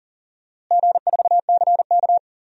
Morse code, heard as a keyed tone, about 40 wpm, 700 Hz. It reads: G4CK
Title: DXCC-100 Call Signs - 4 Characters - Repeated 3x - 40wpm